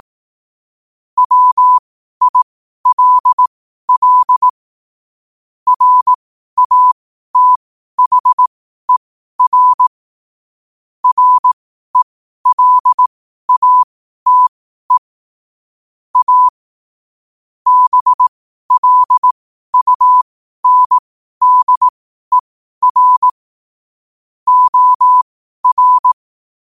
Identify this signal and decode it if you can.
Morse code, heard as a keyed tone, about 18 wpm, spaced as for 12 wpm, 1000 Hz. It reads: WILL RATHER RELATE A BLUNDER OR
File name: qotdfiles / 12wpm